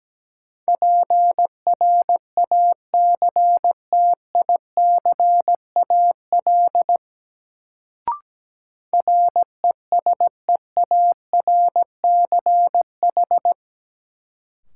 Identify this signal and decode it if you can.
Morse code, heard as a keyed tone, about 17 wpm, 700 Hz. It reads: PRACTICAL RESEARCH
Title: Sentences from Top 1400 Words - New Word Drill Alt - 17wpm